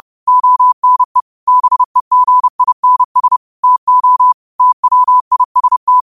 Morse code, heard as a keyed tone, about 30 wpm, 1000 Hz. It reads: ONE BEGINS TO TWIST